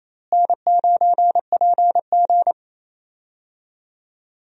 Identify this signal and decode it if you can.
Morse code, heard as a keyed tone, about 28 wpm, 700 Hz. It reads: N9PG
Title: USA Call Signs - 4 Characters - Repeated 2x - 28wpm